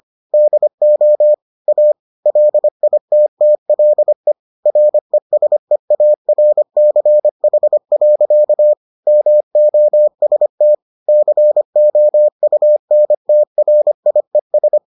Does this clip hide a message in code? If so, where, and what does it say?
Yes: Morse code from 0.3 to 14.8 s, heard as a keyed tone, about 25 wpm, 600 Hz. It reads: DO A LITTLE RESEARCH. MOST COUNTRIES